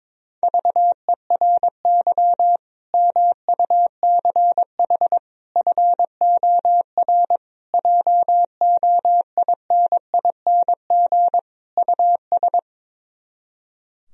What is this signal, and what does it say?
Morse code, heard as a keyed tone, about 22 wpm, 700 Hz. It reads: VERY MUCH FOR JOINING US
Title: Speed-Racing - Sets of 5 Words 22wpm